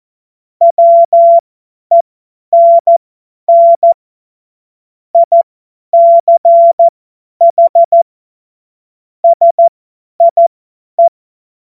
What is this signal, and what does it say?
Morse code, heard as a keyed tone, about 14 wpm, spaced as for 10 wpm, 680 Hz. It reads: WENN ICH SIE
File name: Morsekurs / Witze / Witz 13. Dezember 2022 50 F70